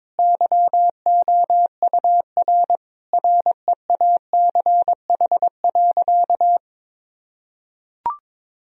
Morse code, heard as a keyed tone, about 22 wpm, 700 Hz. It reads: YOUR REACH.
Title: Sentences from Top 1000 Words - Review Part 2 - 22wpm